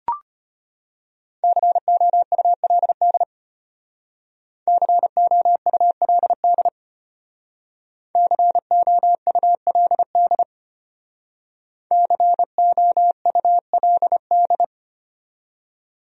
Morse code, heard as a keyed tone, about 32 wpm, 700 Hz. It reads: COULD COULD COULD COULD